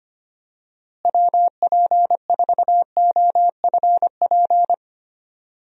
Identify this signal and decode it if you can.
Morse code, heard as a keyed tone, about 25 wpm, 700 Hz. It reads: WP4OFP
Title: USA Call Signs - 6 Characters - Repeated 2x - 25wpm